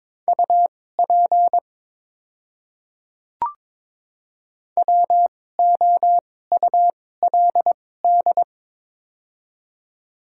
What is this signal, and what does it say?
Morse code, heard as a keyed tone, about 22 wpm, spaced as for 16 wpm, 700 Hz. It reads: UP WOULD
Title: 2x Character Spacing - Top 300 Words - 22wpm